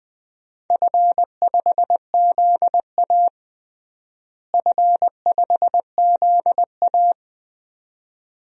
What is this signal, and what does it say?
Morse code, heard as a keyed tone, about 20 wpm, 700 Hz. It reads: F5ZA F5ZA
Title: DXCC-100 Call Signs - 4 Characters - Repeated 2x - 20wpm